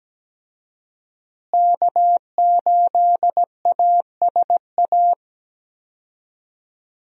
Morse code, heard as a keyed tone, about 17 wpm, 700 Hz. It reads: K8ASA
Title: USA Call Signs - 5 Characters - Repeated 3x - 17wpm